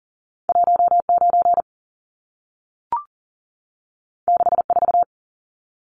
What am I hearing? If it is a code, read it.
Morse code, heard as a keyed tone, about 40 wpm, 700 Hz. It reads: 19 64
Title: Numbers - 2-Digit 40wpm